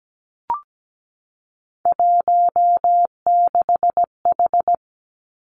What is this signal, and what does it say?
Morse code, heard as a keyed tone, about 17 wpm, 700 Hz. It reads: 16H
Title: Three Challenging Letter-Number Combinations 17wpm